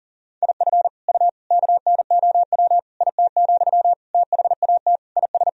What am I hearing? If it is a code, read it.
Morse code, heard as a keyed tone, about 40 wpm, 700 Hz. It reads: IF U KNOW IT, THAT IS
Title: Sentences from Top 400 Words and Abbreviations 40wpm